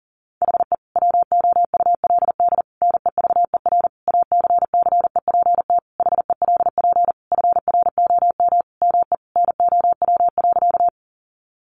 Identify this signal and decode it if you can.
Morse code, heard as a keyed tone, about 40 wpm, 700 Hz. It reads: HE WOULD NEVER ACCEPT HELP FROM ME NOW.
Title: Sentences from Top 1300 Words 40wpm